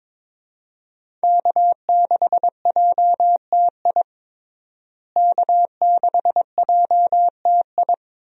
Morse code, heard as a keyed tone, about 22 wpm, 700 Hz. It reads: K6JTI K6JTI
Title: USA Call Signs - 5 Characters - Repeated 3x - 22wpm